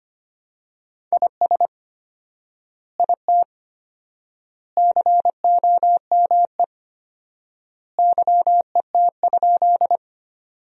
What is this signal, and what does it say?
Morse code, heard as a keyed tone, about 25 wpm, 700 Hz. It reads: IS IT COME YET?
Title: 4x Word-Spacing - Sentences from Top 500 Words - 25wpm